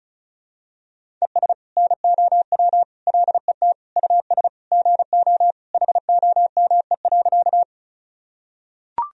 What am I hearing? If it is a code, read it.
Morse code, heard as a keyed tone, about 36 wpm, 700 Hz. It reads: ES NOW LET US GO HOME.
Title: Sentences from Top 300 Words and Abbreviations 35wpm